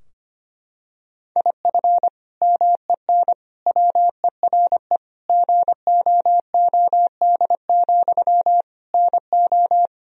Morse code, heard as a keyed tone, about 25 wpm, 700 Hz. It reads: IF MEN WERE GOOD, NO
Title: Sentences from Top 1000 Words - Review Part 3 - 25wpm